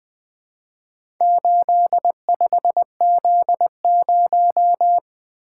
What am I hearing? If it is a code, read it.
Morse code, heard as a keyed tone, about 20 wpm, 700 Hz. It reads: 85Z0
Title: Four Letter-Number Combinations - 20wpm